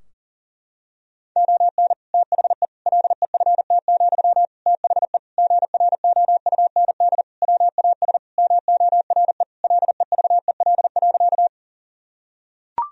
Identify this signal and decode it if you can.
Morse code, heard as a keyed tone, about 40 wpm, 700 Hz. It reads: ON THE LEFT, THE GROUND WAS MORE LEVEL.
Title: Sentences from Top 1000 Words - Review Part 4 - 40wpm